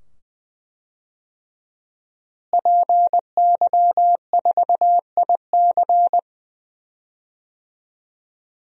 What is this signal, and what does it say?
Morse code, heard as a keyed tone, about 20 wpm, 700 Hz. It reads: PY4IC
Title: DXCC-100 Call Signs - 5 Characters - Repeated 2x - 20wpm